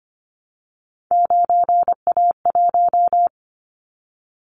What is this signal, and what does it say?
Morse code, heard as a keyed tone, about 25 wpm, 700 Hz. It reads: 9A1
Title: Callsign Prefixes for Top 25 DXCC - Repeated 3x - 25wpm